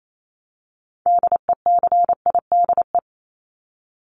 Morse code, heard as a keyed tone, about 28 wpm, 700 Hz. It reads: DECIDE
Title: Top 500 Words Repeated 28wpm